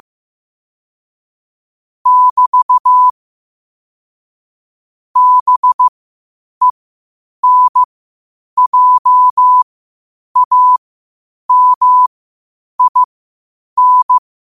Morse code, heard as a keyed tone, about 15 wpm, spaced as for 8 wpm, 1000 Hz. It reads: <BT> BENJAMIN
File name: qotdfiles / 8wpm